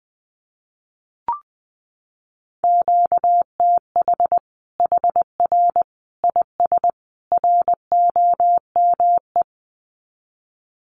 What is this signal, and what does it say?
Morse code, heard as a keyed tone, about 20 wpm, 700 Hz. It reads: QTH HR IS ROME
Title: Common QSO Phrases 20wpm